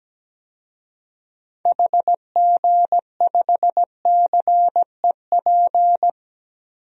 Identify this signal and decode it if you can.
Morse code, heard as a keyed tone, about 17 wpm, 700 Hz. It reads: HG5CEP
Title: DXCC-100 Call Signs - 6 Characters - 17wpm